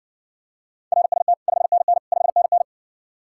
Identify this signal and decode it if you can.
Morse code, heard as a keyed tone, about 60 wpm, 700 Hz. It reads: RST 5NN 5NN